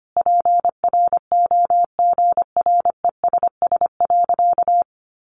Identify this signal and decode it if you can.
Morse code, heard as a keyed tone, about 25 wpm, 700 Hz. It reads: PROGRESS.
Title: Sentences from Top 1400 Words 25wpm